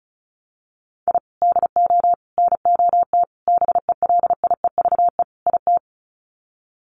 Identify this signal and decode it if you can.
Morse code, heard as a keyed tone, about 34 wpm, 700 Hz. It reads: I DO NOT BELIEVE IT
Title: Sets of 5 Words 35wpm